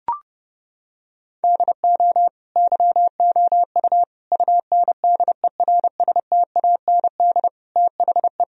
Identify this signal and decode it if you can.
Morse code, heard as a keyed tone, about 30 wpm, 700 Hz. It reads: DO YOU UNDERSTAND THE
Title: Sentences from Top 1000 Words - Review Part 3 - 30wpm